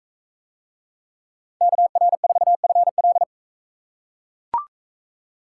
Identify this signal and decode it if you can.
Morse code, heard as a keyed tone, about 41 wpm, 700 Hz. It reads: KR4FL